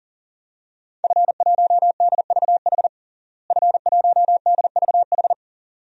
Morse code, heard as a keyed tone, about 37 wpm, 700 Hz. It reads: F1DVH F1DVH
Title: DXCC-100 Call Signs - 5 Characters - Repeated 2x - 40wpm